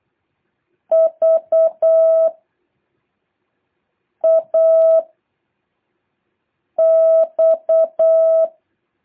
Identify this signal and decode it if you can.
Morse code, heard as a keyed tone, about 8 wpm, 650 Hz. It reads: V A X
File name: jh8cbh / 2023cw